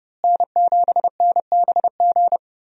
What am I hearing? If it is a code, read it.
Morse code, heard as a keyed tone, about 30 wpm, 700 Hz. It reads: N7NBG
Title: USA Call Signs - 5 Characters - Repeated 2x - 30wpm